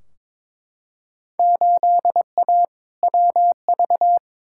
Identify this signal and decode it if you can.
Morse code, heard as a keyed tone, about 22 wpm, 700 Hz. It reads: 8A WV
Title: Field Day Exchanges Encore - 22wpm